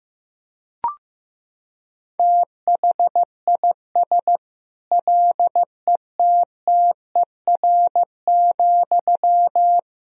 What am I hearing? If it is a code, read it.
Morse code, heard as a keyed tone, about 16 wpm, 700 Hz. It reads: THIS LETTER,
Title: Sentences from Top 1000 Words - Review Part 4 - 15wpm